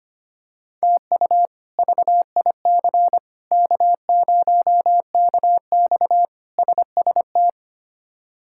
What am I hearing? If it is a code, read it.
Morse code, heard as a keyed tone, about 25 wpm, 700 Hz. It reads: TU VIC K0KX SST
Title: K1USN SST QSO Practice - 25wpm